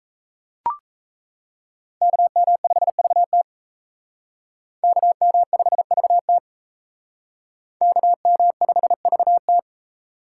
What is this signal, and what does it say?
Morse code, heard as a keyed tone, about 37 wpm, 700 Hz. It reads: KM5VT KM5VT KM5VT